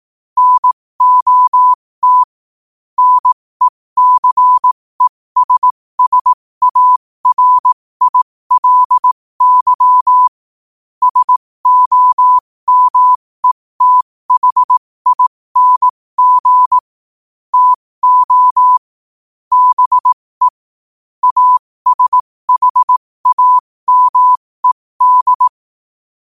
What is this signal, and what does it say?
Morse code, heard as a keyed tone, about 18 wpm, spaced as for 15 wpm, 1000 Hz. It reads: NOT NECESSARILY SOMETHING TO BE ASHAMED
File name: qotdfiles / 15wpm